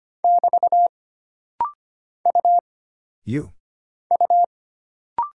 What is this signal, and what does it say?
Morse code, heard as a keyed tone, about 25 wpm, 700 Hz. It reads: <BT> U U
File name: Single Character (Letter:Number:Punctuation:Prosign) - Rapid-Fire - 25wpm